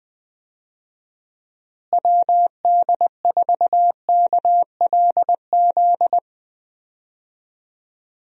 Morse code, heard as a keyed tone, about 20 wpm, 700 Hz. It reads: WD4KLZ
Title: USA Call Signs - 6 Characters - Repeated 3x - 20wpm